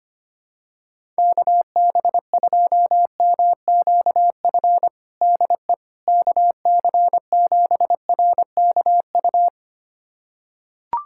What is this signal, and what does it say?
Morse code, heard as a keyed tone, about 25 wpm, 700 Hz. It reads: KB2MQF DE KC7RKU
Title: USA Callsign DE Callsign 25wpm